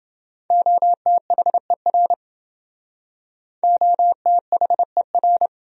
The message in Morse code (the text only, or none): OTHER OTHER